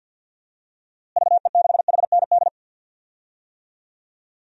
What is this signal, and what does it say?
Morse code, heard as a keyed tone, about 51 wpm, 700 Hz. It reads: VE6HND
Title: DXCC-100 Call Signs - 6 Characters - Repeated 2x - 50wpm